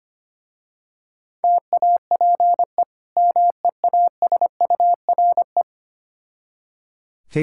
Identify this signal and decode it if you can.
Morse code, heard as a keyed tone, about 25 wpm, 700 Hz. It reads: TAPE MEASURE
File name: practice / 25wpm / Difficult Words with Hints - 25wpm